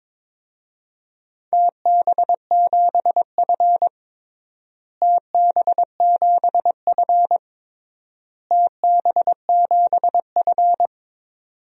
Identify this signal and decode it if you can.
Morse code, heard as a keyed tone, about 21 wpm, 700 Hz. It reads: TB7F TB7F TB7F
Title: DXCC-100 Call Signs - 4 Characters - Repeated 3x - 22wpm